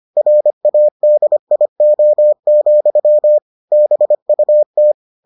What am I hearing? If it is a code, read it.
Morse code, heard as a keyed tone, about 25 wpm, 600 Hz. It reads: RADIO, BUT